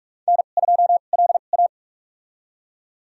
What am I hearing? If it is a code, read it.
Morse code, heard as a keyed tone, about 44 wpm, spaced as for 32 wpm, 700 Hz. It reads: N2LA
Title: USA Call Signs - 4 Characters - 2x Character Spacing - 45wpm